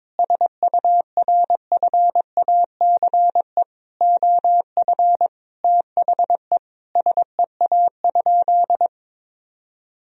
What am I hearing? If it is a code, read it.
Morse code, heard as a keyed tone, about 22 wpm, 700 Hz. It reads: SURFACE OF THE SEA?
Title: Sentences from Top 1000 Words - Review Part 3 - 22wpm